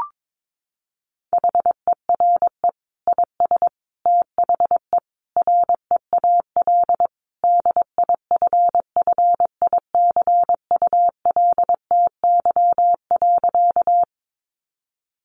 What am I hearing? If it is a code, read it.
Morse code, heard as a keyed tone, about 22 wpm, 700 Hz. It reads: HERE IS THE REAL DIFFICULTY.
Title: Sentences from Top 1500 Words 22wpm